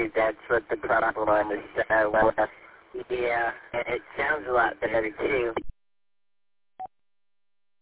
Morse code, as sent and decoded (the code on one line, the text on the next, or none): none